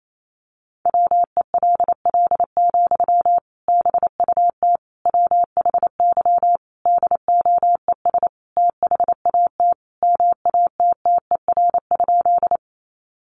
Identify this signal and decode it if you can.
Morse code, heard as a keyed tone, about 28 wpm, 700 Hz. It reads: WELL, BUT WHY DOES THAT MATTER?
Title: Sentences from Top 700 Words 28wpm